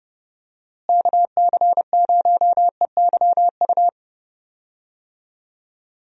Morse code, heard as a keyed tone, about 32 wpm, 700 Hz. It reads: KC0EYU
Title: USA Call Signs - 6 Characters - Repeated 3x - 30wpm